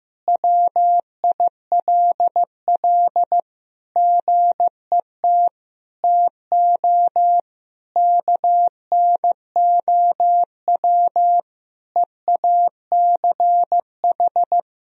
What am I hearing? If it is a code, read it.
Morse code, heard as a keyed tone, about 15 wpm, 700 Hz. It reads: WILL GET TO KNOW EACH
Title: Sentences from Top 1000 Words - Review Part 3 - 15wpm